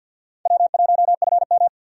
Morse code, heard as a keyed tone, about 46 wpm, 700 Hz. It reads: W1FM
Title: USA Call Signs - 4 Characters - 50wpm